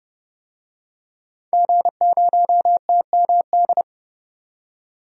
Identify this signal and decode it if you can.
Morse code, heard as a keyed tone, about 32 wpm, 700 Hz. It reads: G0TMD
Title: DXCC-100 Call Signs - 5 Characters - 30wpm